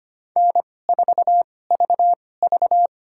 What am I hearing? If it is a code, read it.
Morse code, heard as a keyed tone, about 25 wpm, spaced as for 18 wpm, 700 Hz. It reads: N4VV
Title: USA Call Signs - 2x Character Spacing - 25wpm